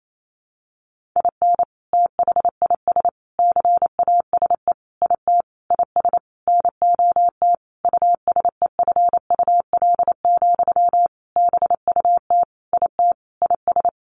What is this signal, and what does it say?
Morse code, heard as a keyed tone, about 28 wpm, 700 Hz. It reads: IN THIS CASE IT IS NOT USEFUL, BUT IT IS
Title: Sentences from Top 1400 Words 28wpm